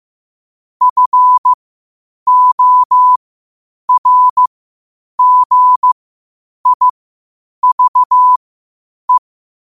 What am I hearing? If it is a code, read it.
Morse code, heard as a keyed tone, about 15 wpm, spaced as for 8 wpm, 1000 Hz. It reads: FORGIVE
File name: qotdfiles / 8wpm